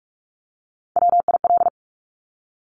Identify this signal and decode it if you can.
Morse code, heard as a keyed tone, about 46 wpm, 700 Hz. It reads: WIL
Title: Top Three Letter Combinations in Words 45wpm